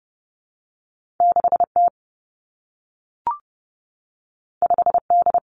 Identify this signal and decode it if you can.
Morse code, heard as a keyed tone, about 30 wpm, 700 Hz. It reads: 6 T 5 D
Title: Two Challenging Letter-Number Combinations 30wpm